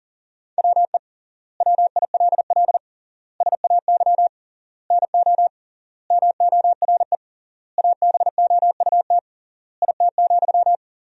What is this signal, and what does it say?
Morse code, heard as a keyed tone, about 40 wpm, 700 Hz. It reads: WE WILL SAY NO MORE ABOUT IT,
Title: 3x Word-Spacing - Sentences from Top 200 Words - 40wpm